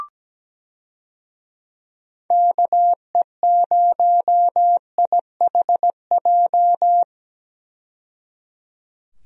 Morse code, heard as a keyed tone, about 17 wpm, 700 Hz. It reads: KE0IHJ
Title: USA Call Signs - 6 Characters - 17wpm